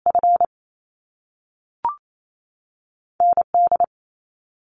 Morse code, heard as a keyed tone, about 28 wpm, 700 Hz. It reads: F ND
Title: Top Two Letter Combinations in Words 28wpm